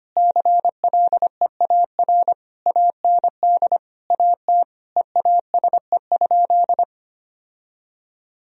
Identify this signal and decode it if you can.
Morse code, heard as a keyed tone, about 25 wpm, 700 Hz. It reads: CLEAR AND AT EASE?
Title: Sentences from Top 1000 Words - Review Part 1 - 25wpm